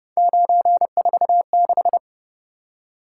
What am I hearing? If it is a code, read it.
Morse code, heard as a keyed tone, about 30 wpm, 700 Hz. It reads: 946